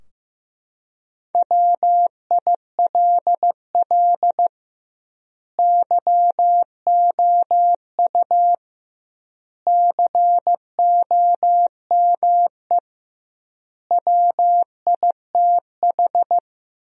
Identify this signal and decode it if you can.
Morse code, heard as a keyed tone, about 16 wpm, 700 Hz. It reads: WILL YOU COME WITH
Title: 2x Word-Spacing - Sentences from Top 200 Words - 15wpm